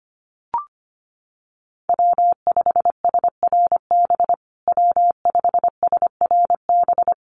Morse code, heard as a keyed tone, about 25 wpm, 700 Hz. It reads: W5SRB W5SRB
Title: USA Call Signs Repeated 3x 25wpm